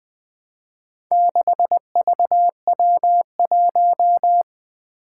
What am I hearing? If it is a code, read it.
Morse code, heard as a keyed tone, about 20 wpm, 700 Hz. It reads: 6VW1
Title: Four Challenging Letter-Number Combinations 20wpm